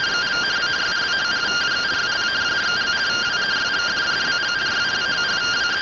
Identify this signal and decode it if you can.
Two-tone teleprinter signal, about 170 Hz shift.
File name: findings / CT AXIAL -111.gen-03 rtty rx